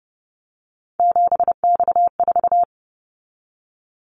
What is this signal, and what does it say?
Morse code, heard as a keyed tone, about 30 wpm, 700 Hz. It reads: 7X4
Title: Callsign Prefixes for Top 100 DXCC - 30wpm